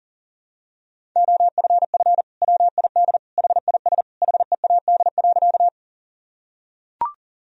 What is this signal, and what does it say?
Morse code, heard as a keyed tone, about 40 wpm, 700 Hz. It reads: OFF WID HIS HEAD.
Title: Sentences from Top 300 Words and Abbreviations 40wpm